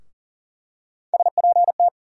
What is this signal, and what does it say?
Morse code, heard as a keyed tone, about 39 wpm, 700 Hz. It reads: SPT
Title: USA Call Sign Postfixes - 3 Letters 40wpm